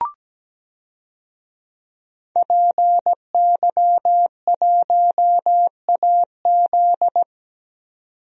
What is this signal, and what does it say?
Morse code, heard as a keyed tone, about 17 wpm, 700 Hz. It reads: PY1AZ
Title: DXCC-100 Call Signs - 5 Characters - 17wpm